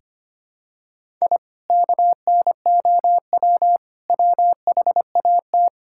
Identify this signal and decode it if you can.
Morse code, heard as a keyed tone, about 25 wpm, 700 Hz. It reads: I KNOW WHAT
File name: practice / 25wpm / Sentences with 4-Word Phrases - 25wpm